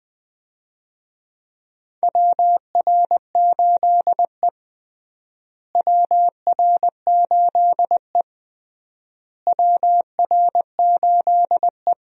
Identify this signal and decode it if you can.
Morse code, heard as a keyed tone, about 21 wpm, 700 Hz. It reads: WR8E WR8E WR8E
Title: USA Call Signs - 4 Characters - Repeated 3x - 20wpm